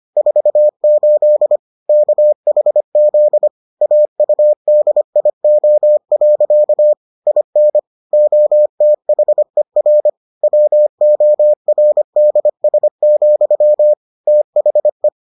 Morse code, heard as a keyed tone, about 25 wpm, 600 Hz. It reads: 48 KHZ AUDIO. IN OTHER WORDS, THE